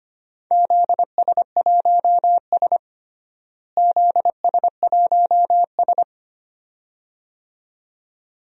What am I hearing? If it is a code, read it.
Morse code, heard as a keyed tone, about 25 wpm, 700 Hz. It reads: ZS1S ZS1S
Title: DXCC-100 Call Signs - 4 Characters - Repeated 2x - 25wpm